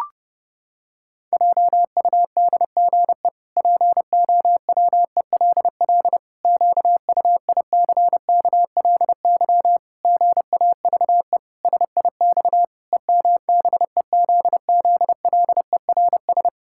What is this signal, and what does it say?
Morse code, heard as a keyed tone, about 30 wpm, 700 Hz. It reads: JUDGE POWELL QUICKLY GAVE SIX EMBEZZLERS
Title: Pangram Challenge v2 - 30wpm